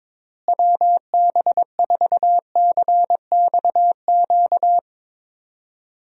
Morse code, heard as a keyed tone, about 22 wpm, 700 Hz. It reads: WB4CXQ
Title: USA Call Signs - 6 Characters - Repeated 2x - 22wpm